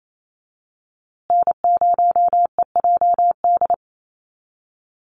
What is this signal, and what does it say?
Morse code, heard as a keyed tone, about 28 wpm, 700 Hz. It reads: N0EJD
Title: USA Call Signs Repeated 2x 28wpm